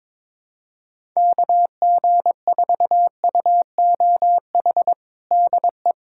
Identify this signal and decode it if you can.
Morse code, heard as a keyed tone, about 22 wpm, 700 Hz. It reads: KG4UOH DE
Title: USA Callsign DE Callsign - 6 Characters - 22wpm